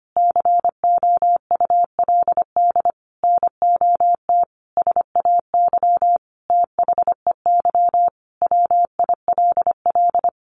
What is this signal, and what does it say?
Morse code, heard as a keyed tone, about 25 wpm, 700 Hz. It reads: COULD NOT SAY THEY WILL
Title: Sentences from Top 1100 Words 25wpm